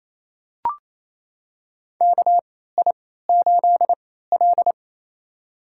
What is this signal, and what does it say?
Morse code, heard as a keyed tone, about 28 wpm, spaced as for 16 wpm, 700 Hz. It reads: KI8L
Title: USA Call Signs - 4 Characters - 3x Character Spacing - 28wpm